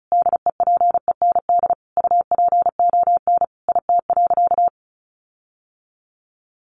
Morse code, heard as a keyed tone, about 34 wpm, 700 Hz. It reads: DEPEND UPON IT.
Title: Sentences from Top 1300 Words 35wpm